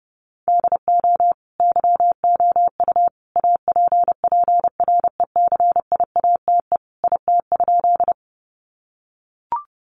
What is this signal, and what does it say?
Morse code, heard as a keyed tone, about 28 wpm, 700 Hz. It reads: DO YOU APPRECIATE IT?
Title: Sentences from Top 1300 Words 30wpm